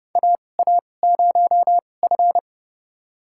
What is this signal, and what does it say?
Morse code, heard as a keyed tone, about 30 wpm, spaced as for 22 wpm, 700 Hz. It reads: AA0F
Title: USA Call Signs - 4 Characters - 2x Character Spacing - 30wpm